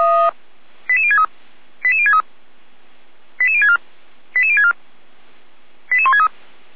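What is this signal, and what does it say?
Five-tone selective call (ZVEI), heard as quick runs of five tones, tones of about 70 ms. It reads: 88063 88063 88064 88064 88163
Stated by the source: ZVEI five-tone selective call received on an SDR.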